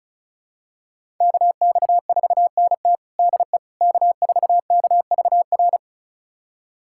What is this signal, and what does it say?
Morse code, heard as a keyed tone, about 36 wpm, 700 Hz. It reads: KX4NT DE K4KVR
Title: USA Callsign DE Callsign - 5 Characters - 35wpm